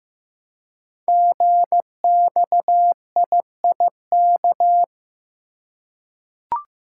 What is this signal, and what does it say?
Morse code, heard as a keyed tone, about 15 wpm, 700 Hz. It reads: GXIIK
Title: Code Groups - 5 Letters - 15wpm